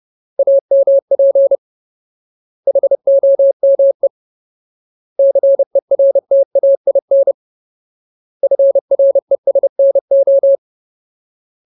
Morse code, heard as a keyed tone, about 30 wpm, 550 Hz. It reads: AMP HOME CERTAIN FRESNO
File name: Ultimate Word List / 30WPM Ultimate